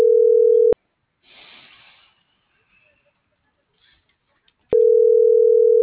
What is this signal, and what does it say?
Ringback tone (North America), 440 + 480 Hz.